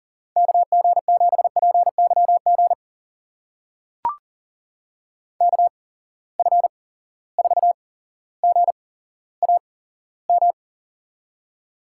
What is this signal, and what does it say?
Morse code, heard as a keyed tone, about 40 wpm, spaced as for 17 wpm, 700 Hz. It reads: KG7PYG KF4GAM
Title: USA Call Signs - 6 Characters - 8x Character Spacing - 40wpm